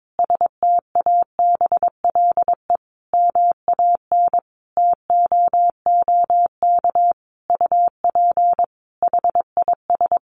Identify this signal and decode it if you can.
Morse code, heard as a keyed tone, about 22 wpm, 700 Hz. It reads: STABLE MAN TOOK UP HIS
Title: Sentences from Top 1500 Words 22wpm